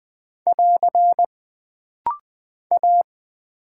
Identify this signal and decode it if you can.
Morse code, heard as a keyed tone, about 20 wpm, 700 Hz. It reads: <AR> A
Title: Single Character (Letter:Number:Punctuation:Prosign) - Rapid-Fire - 20wpm